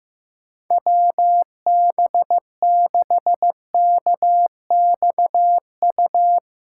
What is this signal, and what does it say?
Morse code, heard as a keyed tone, about 16 wpm, 700 Hz. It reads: WB6KXU
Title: USA Callsign DE Callsign 15wpm